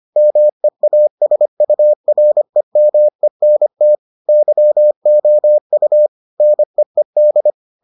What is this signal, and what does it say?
Morse code, heard as a keyed tone, about 25 wpm, 600 Hz. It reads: MEASUREMENT YOU NEED